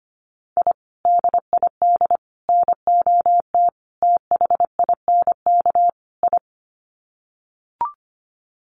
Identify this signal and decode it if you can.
Morse code, heard as a keyed tone, about 25 wpm, 700 Hz. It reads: I DID NOT THINK I
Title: Sets of 5 Words Encore 25wpm